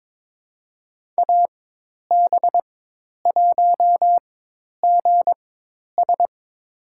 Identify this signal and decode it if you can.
Morse code, heard as a keyed tone, about 22 wpm, spaced as for 10 wpm, 700 Hz. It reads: AB1GS
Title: USA Call Signs - 5 Characters - 4x Character Spacing - 22wpm